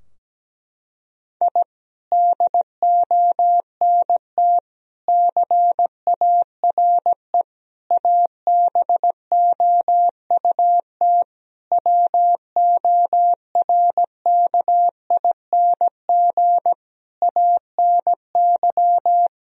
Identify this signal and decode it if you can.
Morse code, heard as a keyed tone, about 17 wpm, 700 Hz. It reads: I DONT CARE ABOUT WORKING ANY
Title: Sentences from Top 1000 Words - Review Part 4 - 17wpm